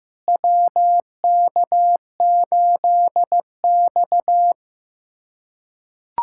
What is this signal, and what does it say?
Morse code, heard as a keyed tone, about 15 wpm, 700 Hz. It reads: WK8X
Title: USA Callsign DE Callsign - 4 Characters - 15wpm